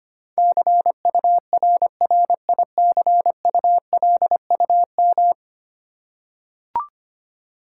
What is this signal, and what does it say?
Morse code, heard as a keyed tone, about 25 wpm, 700 Hz. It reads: CURRICULUM
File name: 10 Letter Words - First 3 Letters Given - 25wpm